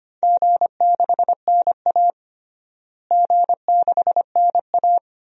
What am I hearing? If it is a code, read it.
Morse code, heard as a keyed tone, about 25 wpm, 700 Hz. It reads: G6NA G6NA
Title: DXCC-100 Call Signs - 4 Characters - Repeated 2x - 25wpm